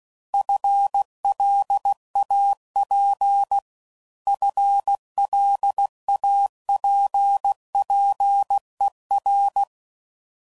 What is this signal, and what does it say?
Morse code, heard as a keyed tone, about 16 wpm, 800 Hz. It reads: FLAP FLAPPER
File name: Prefixes / FLs & THs / FL Runs